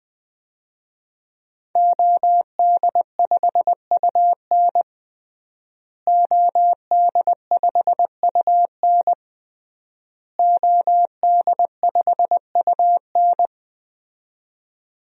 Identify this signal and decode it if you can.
Morse code, heard as a keyed tone, about 20 wpm, 700 Hz. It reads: OD5UN OD5UN OD5UN
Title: DXCC-100 Call Signs - 5 Characters - Repeated 3x - 20wpm